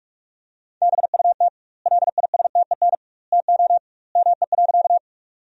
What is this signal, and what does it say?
Morse code, heard as a keyed tone, about 41 wpm, 700 Hz. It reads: BUT LISTEN TO ME.
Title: 2x Word-Spacing - Sentences from Top 400 Words - 45wpm